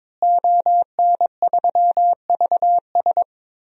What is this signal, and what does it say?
Morse code, heard as a keyed tone, about 21 wpm, 700 Hz. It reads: ON3VS
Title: DXCC-100 Call Signs - 5 Characters - Repeated 3x - 22wpm